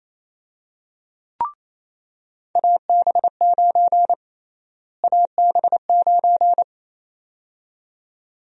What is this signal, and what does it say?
Morse code, heard as a keyed tone, about 28 wpm, 700 Hz. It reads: AB9 AB9
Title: USA Call Sign Prefixes - Repeated 2x - 28wpm